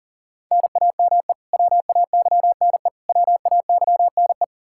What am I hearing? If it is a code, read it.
Morse code, heard as a keyed tone, about 40 wpm, 700 Hz. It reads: NAME WAYNE WAYNE
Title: Practice QSO - Exchange 1 - Beginning - 40wpm